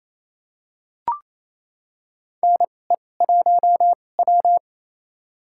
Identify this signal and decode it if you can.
Morse code, heard as a keyed tone, about 28 wpm, spaced as for 20 wpm, 700 Hz. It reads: NE1W
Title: USA Call Signs - 4 Characters - 2x Character Spacing - 28wpm